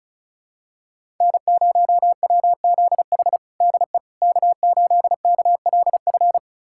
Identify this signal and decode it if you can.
Morse code, heard as a keyed tone, about 36 wpm, 700 Hz. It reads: N0WZH DE K8KLF
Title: USA Callsign DE Callsign - 5 Characters - 35wpm